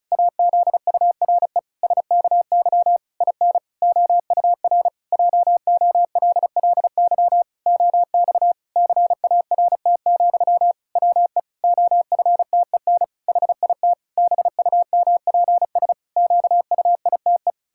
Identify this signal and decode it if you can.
Morse code, heard as a keyed tone, about 34 wpm, 700 Hz. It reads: AZURE SKY IN OUR JOLLY OX CART, WE OFTEN HIT BUMPS QUITE
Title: Pangram Challenge v2 - 35wpm